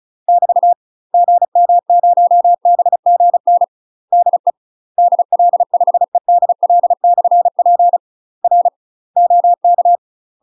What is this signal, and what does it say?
Morse code, heard as a keyed tone, about 35 wpm, 700 Hz. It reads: <BT> GM0BGN DE DL5EDL/P R OK
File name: UK MASTER QSOs TRAINING / 35WPM / UK MASTER QSOs 0034